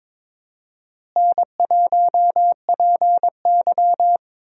Morse code, heard as a keyed tone, about 22 wpm, 700 Hz. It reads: N1PY
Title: USA Call Signs - 4 Characters - Repeated 2x - 22wpm